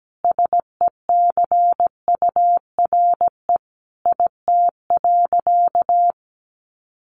Morse code, heard as a keyed tone, about 17 wpm, 700 Hz. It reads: SECURE IT.
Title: Sentences from Top 1200 Words 17wpm